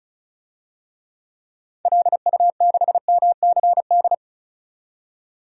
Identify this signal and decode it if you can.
Morse code, heard as a keyed tone, about 36 wpm, 700 Hz. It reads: LU6MCD
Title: DXCC-100 Call Signs - 6 Characters - 35wpm